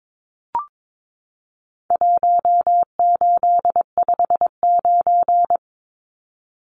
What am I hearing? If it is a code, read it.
Morse code, heard as a keyed tone, about 22 wpm, 700 Hz. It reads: 1859
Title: Numbers - 4-Digit 22wpm